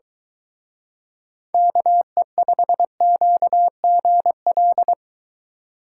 Morse code, heard as a keyed tone, about 23 wpm, 700 Hz. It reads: KE5QGL